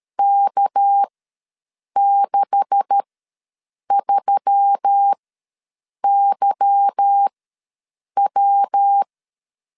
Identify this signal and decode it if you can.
Morse code, heard as a keyed tone, about 12 wpm, spaced as for 7 wpm, 800 Hz. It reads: K63YW
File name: LEZIONI / 10 SIGNS / 10A SLASH ws 12wpm 3443